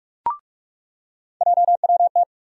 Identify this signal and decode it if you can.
Morse code, heard as a keyed tone, about 46 wpm, 700 Hz. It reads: JWT
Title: USA Call Sign Postfixes - 3 Letters 45wpm